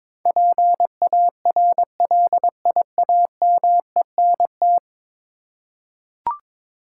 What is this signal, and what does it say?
Morse code, heard as a keyed tone, about 21 wpm, 700 Hz. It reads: PARLIAMENT
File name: practice / 22wpm / 10 Letter Words - First 3 Letters Given - 22wpm